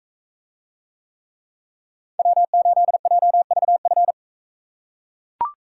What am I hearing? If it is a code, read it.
Morse code, heard as a keyed tone, about 44 wpm, 700 Hz. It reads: W8JVF